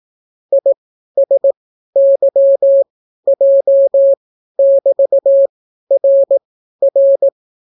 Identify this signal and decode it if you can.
Morse code, heard as a keyed tone, about 18 wpm, spaced as for 12 wpm, 550 Hz. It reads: ISYJ<BT>RR